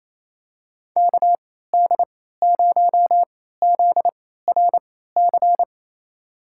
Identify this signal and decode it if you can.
Morse code, heard as a keyed tone, about 28 wpm, spaced as for 16 wpm, 700 Hz. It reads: KD0ZRC
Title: USA Call Signs - 3x Character Spacing - 28wpm